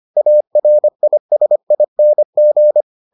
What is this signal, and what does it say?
Morse code, heard as a keyed tone, about 25 wpm, 600 Hz. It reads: ARISING